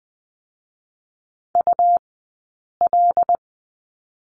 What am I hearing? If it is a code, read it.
Morse code, heard as a keyed tone, about 20 wpm, 700 Hz. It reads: U L
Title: 099 Lesson 17d - Letter u - Two Characters 20wpm